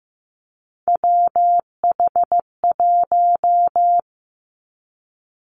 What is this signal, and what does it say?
Morse code, heard as a keyed tone, about 15 wpm, 700 Hz. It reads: WH1
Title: Three Challenging Letter-Number Combinations 15wpm